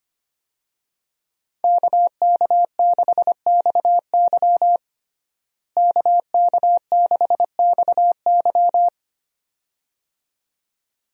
Morse code, heard as a keyed tone, about 25 wpm, 700 Hz. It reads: KK6XY KK6XY
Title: USA Call Signs - 5 Characters - Repeated 2x - 25wpm